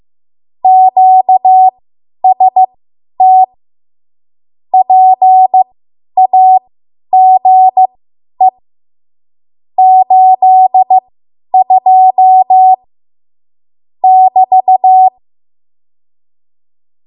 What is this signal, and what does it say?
Morse code, heard as a keyed tone, about 15 wpm, spaced as for 10 wpm, 750 Hz. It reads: QST PAGE 82 <BT>
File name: Morse / 10WPM